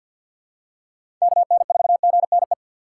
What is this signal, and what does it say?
Morse code, heard as a keyed tone, about 47 wpm, 700 Hz. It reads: KN4GNE
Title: USA Call Signs - 6 Characters - 6x Character Spacing - 50wpm